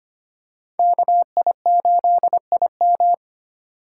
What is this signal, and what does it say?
Morse code, heard as a keyed tone, about 25 wpm, 700 Hz. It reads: KI8IM
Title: USA Call Signs - 3x Character Spacing - 25wpm